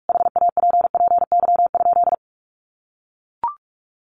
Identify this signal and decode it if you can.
Morse code, heard as a keyed tone, about 44 wpm, 700 Hz. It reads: HAPPY?
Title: Sentences from Top 1000 Words 45wpm